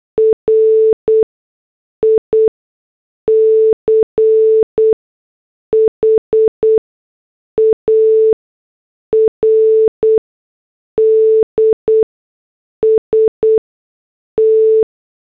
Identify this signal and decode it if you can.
Morse code, heard as a keyed tone, about 8 wpm, spaced as for 6 wpm, 440 Hz. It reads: RICHARD ST